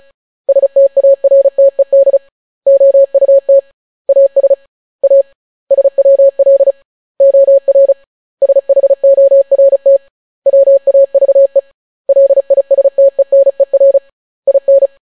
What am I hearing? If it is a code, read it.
Morse code, heard as a keyed tone, about 36 wpm, 550 Hz. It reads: STARTED OUT AS A SWL OR SHORT WAVE LISTENER IN